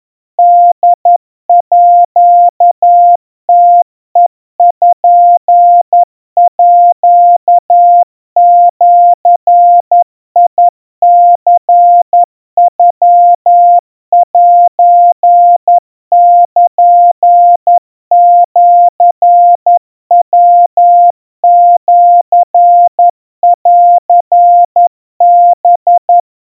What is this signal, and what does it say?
Morse code (Japanese Wabun code), heard as a keyed tone, about 11 wpm, 700 Hz. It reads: ホームページニノセルシヤシンハ
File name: kawa / Morse sample / kamyu 40 cpm 700 Hz Crispy